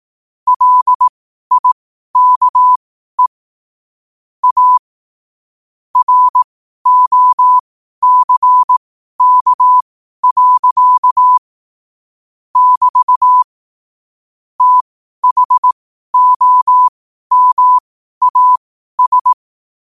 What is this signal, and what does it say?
Morse code, heard as a keyed tone, about 18 wpm, spaced as for 12 wpm, 1000 Hz. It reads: LIKE A ROCK. <BT> THOMAS
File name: qotdfiles / 12wpm